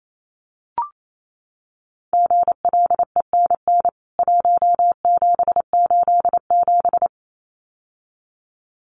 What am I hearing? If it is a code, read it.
Morse code, heard as a keyed tone, about 28 wpm, 700 Hz. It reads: GLENN 1787
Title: CWT Exchange 28wpm